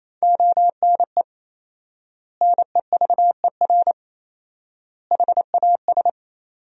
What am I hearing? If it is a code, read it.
Morse code, heard as a keyed tone, about 28 wpm, 700 Hz. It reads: ONE NEVER HAS
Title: 4x Word-Spacing - Sentences from Top 300 Words - 28wpm